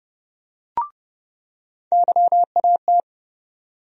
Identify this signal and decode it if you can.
Morse code, heard as a keyed tone, about 30 wpm, 700 Hz. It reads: YAT
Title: USA Call Sign Postfixes - 3 Letters 30wpm